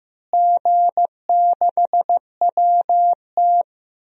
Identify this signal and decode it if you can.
Morse code, heard as a keyed tone, about 15 wpm, 700 Hz. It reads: G6WT
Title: DXCC-100 Call Signs - 4 Characters - Repeated 2x - 15wpm